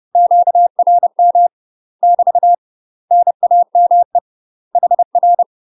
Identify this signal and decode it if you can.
Morse code, heard as a keyed tone, about 30 wpm, 700 Hz. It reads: QRM <BT> NAME HR
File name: UK MASTER QSOs TRAINING / 30WPM / UK MASTER QSOs 0013